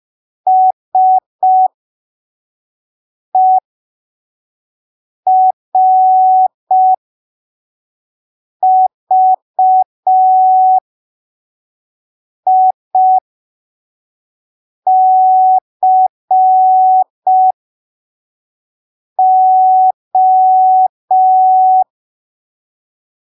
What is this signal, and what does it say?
Morse code, heard as a keyed tone, about 5 wpm, spaced as for 3 wpm, 750 Hz. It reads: SERVICO
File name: arquivos / texto / texto 01 5ppm